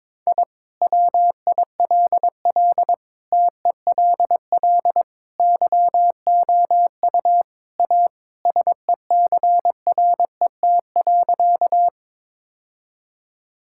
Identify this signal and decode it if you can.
Morse code, heard as a keyed tone, about 21 wpm, 700 Hz. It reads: I WILL TELL YOU A SECRET.
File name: practice / 22wpm / Sentences with 2-Word Phrases - 22wpm